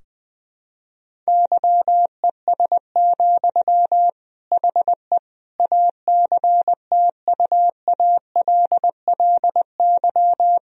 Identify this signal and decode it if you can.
Morse code, heard as a keyed tone, about 20 wpm, 700 Hz. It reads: YES, HE ACTUALLY
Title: Sentences from Top 1000 Words - Review Part 4 - 20wpm